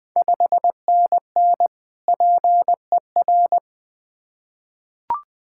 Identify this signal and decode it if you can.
Morse code, heard as a keyed tone, about 20 wpm, 700 Hz. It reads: 5NN PER
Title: Ontario QSO Party Exchange 20wpm